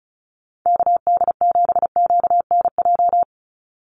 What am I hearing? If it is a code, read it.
Morse code, heard as a keyed tone, about 34 wpm, 700 Hz. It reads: KD7QNJ
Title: USA Call Signs 35wpm